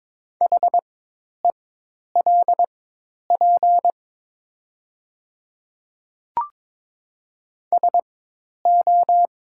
Morse code, heard as a keyed tone, about 22 wpm, spaced as for 10 wpm, 700 Hz. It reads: HELP SO
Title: 4x Character Spacing - Top 200 Words - 22wpm